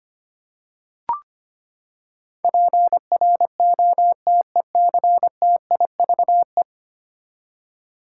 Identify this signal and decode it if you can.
Morse code, heard as a keyed tone, about 25 wpm, 700 Hz. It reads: PROTECTIVE
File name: Ten Letter Words - 25wpm